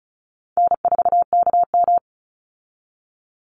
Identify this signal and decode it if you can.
Morse code, heard as a keyed tone, about 34 wpm, 700 Hz. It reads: N4KM
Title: Call Signs Repeated 2x - 35wpm